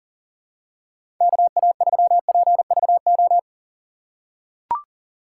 Morse code, heard as a keyed tone, about 40 wpm, 700 Hz. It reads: KA3PVO
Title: USA Call Signs - 6 Characters - 3x Character Spacing - 40wpm